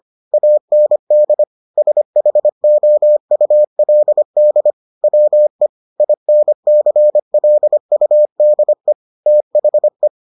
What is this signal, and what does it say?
Morse code, heard as a keyed tone, about 25 wpm, 600 Hz. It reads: AND SHOULD WE INCLUDE THE